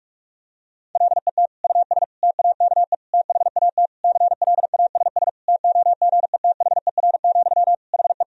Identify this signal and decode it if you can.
Morse code, heard as a keyed tone, about 46 wpm, 700 Hz. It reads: LET US TAKE THAT CLASS TOGETHER, HE
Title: Sentences from Top 1000 Words - Review Part 2 - 45wpm